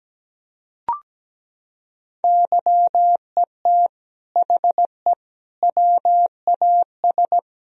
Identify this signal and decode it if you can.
Morse code, heard as a keyed tone, about 17 wpm, 700 Hz. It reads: YET HE WAS
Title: Sentences from Top 1000 Words - Review Part 3 - 17wpm